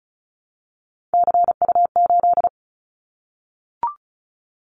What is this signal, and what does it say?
Morse code, heard as a keyed tone, about 36 wpm, 700 Hz. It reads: CU8
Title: Callsign Prefixes for Top 50 DXCC - 35wpm